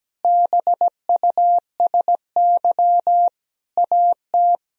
Morse code, heard as a keyed tone, about 17 wpm, 700 Hz. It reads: BUSY AT
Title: Sentences from Top 1000 Words - Review Part 3 - 17wpm